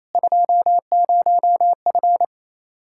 Morse code, heard as a keyed tone, about 28 wpm, 700 Hz. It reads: 20F
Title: Practice QSO - Exchange 2 - Info - 28wpm